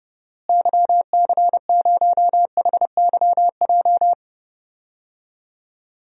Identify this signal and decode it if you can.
Morse code, heard as a keyed tone, about 31 wpm, 700 Hz. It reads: YC0HYJ
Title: DXCC-100 Call Signs - 6 Characters - Repeated 2x - 30wpm